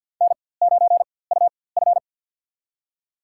Morse code, heard as a keyed tone, about 51 wpm, spaced as for 23 wpm, 700 Hz. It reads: N9UF
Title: USA Call Signs - 4x Character Spacing - 50wpm